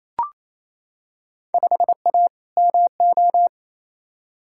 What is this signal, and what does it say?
Morse code, heard as a keyed tone, about 28 wpm, 700 Hz. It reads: 5A MO
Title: Field Day Exchanges Encore - 28wpm